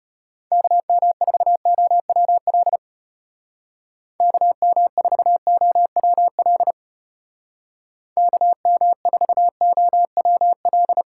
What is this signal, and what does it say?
Morse code, heard as a keyed tone, about 30 wpm, 700 Hz. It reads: KM4OWL KM4OWL KM4OWL
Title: Speed Racing - USA Call Signs 25wpm